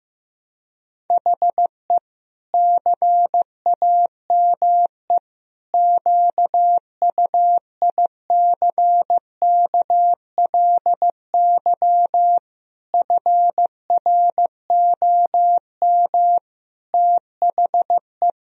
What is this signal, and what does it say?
Morse code, heard as a keyed tone, about 15 wpm, 700 Hz. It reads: HE CAME QUICKLY FROM THE
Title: Sentences from Top 1000 Words - Review Part 1 - 15wpm